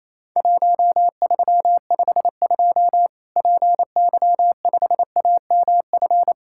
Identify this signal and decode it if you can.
Morse code, heard as a keyed tone, about 27 wpm, 700 Hz. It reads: 1352 PY5AMF
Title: Worked All Europe DX Contest - QTC Exchange - 28wpm